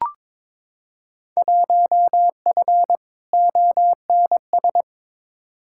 Morse code, heard as a keyed tone, about 22 wpm, 700 Hz. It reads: 1F ONS
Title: Field Day Exchanges Encore - 22wpm